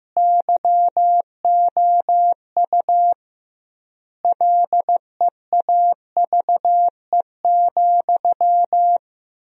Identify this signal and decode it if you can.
Morse code, heard as a keyed tone, about 15 wpm, 700 Hz. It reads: YOU LEAVE,
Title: 2x Word-Spacing - Sentences from Top 400 Words - 15wpm